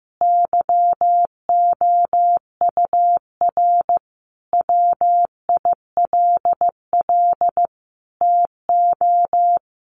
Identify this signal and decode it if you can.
Morse code, heard as a keyed tone, about 14 wpm, 700 Hz. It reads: YOUR WILL TO
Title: Sentences from Top 1500 Words 15wpm